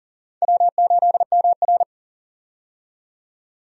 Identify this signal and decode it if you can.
Morse code, heard as a keyed tone, about 40 wpm, 700 Hz. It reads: W8MR
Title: USA Call Signs - 4 Characters - 40wpm